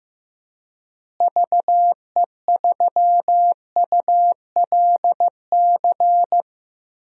Morse code, heard as a keyed tone, about 15 wpm, 700 Hz. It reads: VE3ULC
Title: DXCC-100 Call Signs - 6 Characters - Repeated 2x - 15wpm